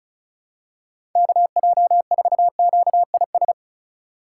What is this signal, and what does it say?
Morse code, heard as a keyed tone, about 36 wpm, 700 Hz. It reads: KJ4QIS
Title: USA Call Signs - 6 Characters - 3x Character Spacing - 35wpm